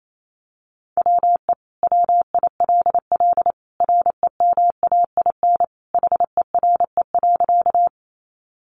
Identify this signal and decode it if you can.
Morse code, heard as a keyed tone, about 28 wpm, 700 Hz. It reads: WE WILL REMAIN HERE.
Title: Sentences from Top 1200 Words 28wpm